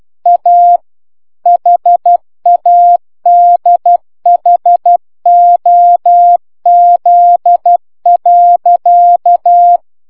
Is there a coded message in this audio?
Yes: Morse code from 0.3 to 9.8 s, heard as a keyed tone, about 12 wpm, 700 Hz. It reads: A HADHOZ.